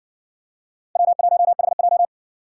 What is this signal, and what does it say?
Morse code, heard as a keyed tone, about 55 wpm, 700 Hz. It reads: W1HJ